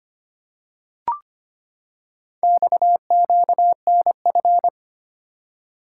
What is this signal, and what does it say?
Morse code, heard as a keyed tone, about 25 wpm, 700 Hz. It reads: XQNF
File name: practice / 25wpm / Four Letter-Number Combinations - 25wpm